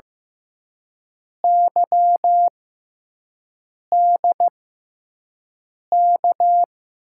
Morse code, heard as a keyed tone, about 15 wpm, spaced as for 5 wpm, 700 Hz. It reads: YDK
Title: USA Call Signs - 6 Characters - 6x Character Spacing - 15wpm